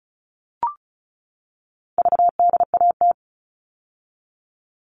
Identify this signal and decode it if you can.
Morse code, heard as a keyed tone, about 34 wpm, 700 Hz. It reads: VDAT
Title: All Cut Numbers - 4-Digit 35wpm